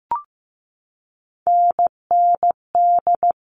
Morse code, heard as a keyed tone, about 15 wpm, 700 Hz. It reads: NND
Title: All Cut Numbers - 3-Digit 15wpm